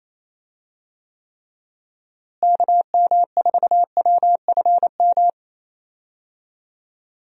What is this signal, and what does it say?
Morse code, heard as a keyed tone, about 28 wpm, 700 Hz. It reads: KM4WFM
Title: USA Call Signs - 6 Characters - Repeated 3x - 28wpm